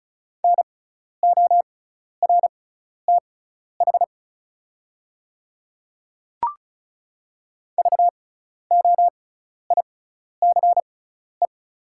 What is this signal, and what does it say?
Morse code, heard as a keyed tone, about 36 wpm, spaced as for 12 wpm, 690 Hz. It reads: NORTH VOICE
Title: 6x Character Spacing - Top 500 Words - 35wpm